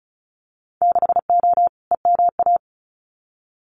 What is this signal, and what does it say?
Morse code, heard as a keyed tone, about 34 wpm, 700 Hz. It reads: 6O EMA
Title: Winter Field Day Exchanges Encore 35wpm